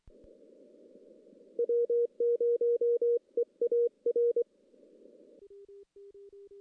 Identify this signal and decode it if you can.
Morse code, heard as a keyed tone, about 23 wpm, 470 Hz. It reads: W0EAR
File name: contests / CW-SS-k5tr- -1600